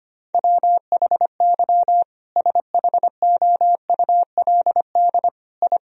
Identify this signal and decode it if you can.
Morse code, heard as a keyed tone, about 25 wpm, 700 Hz. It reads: WHY SHOULD I
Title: Sentences from Top 1000 Words - Review Part 3 - 25wpm